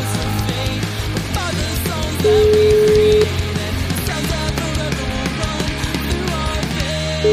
Ringback tone (Europe, 425 Hz), with noise.